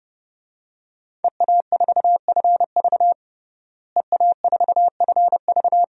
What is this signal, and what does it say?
Morse code, heard as a keyed tone, about 30 wpm, 700 Hz. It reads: EA4FV EA4FV
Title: DXCC-100 Call Signs - 5 Characters - Repeated 2x - 30wpm